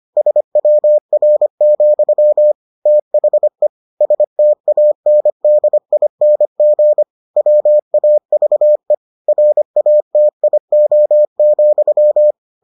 Morse code, heard as a keyed tone, about 25 wpm, 600 Hz. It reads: SWR, THE STANDING WAVE RATIO,